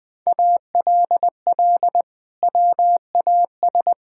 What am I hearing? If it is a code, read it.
Morse code, heard as a keyed tone, about 20 wpm, 700 Hz. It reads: ALL WAS
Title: Sentences from Top 1000 Words - Review Part 6 - 20wpm